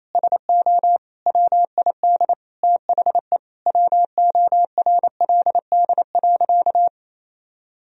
Morse code, heard as a keyed tone, about 28 wpm, 700 Hz. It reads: SO WID THE WORLD.
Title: Sentences from Top 200 Words and Abbreviations 28wpm